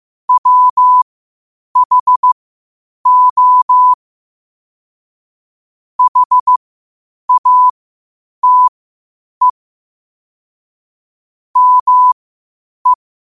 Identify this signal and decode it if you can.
Morse code, heard as a keyed tone, about 15 wpm, spaced as for 8 wpm, 1000 Hz. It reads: WHO HATE ME
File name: qotdfiles / 8wpm